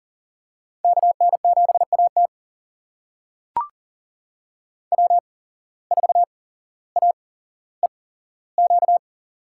Morse code, heard as a keyed tone, about 40 wpm, spaced as for 17 wpm, 700 Hz. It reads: KN7AT W4AEQ
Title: USA Call Signs - 5 Characters - 8x Character Spacing - 40wpm